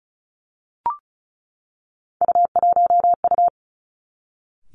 Morse code, heard as a keyed tone, about 36 wpm, 700 Hz. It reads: U1U
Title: Three Challenging Letter-Number Combinations 35wpm